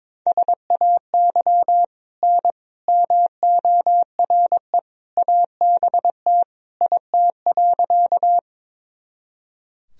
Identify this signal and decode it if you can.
Morse code, heard as a keyed tone, about 22 wpm, 700 Hz. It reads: SAY N MORE ABT IT.
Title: Sentences from Top 200 Words and Abbreviations 22wpm